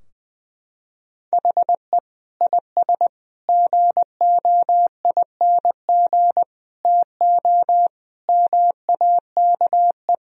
Morse code, heard as a keyed tone, about 20 wpm, 700 Hz. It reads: HE IS GOING TO MAKE
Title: Sentences from Top 1000 Words - Review Part 4 - 20wpm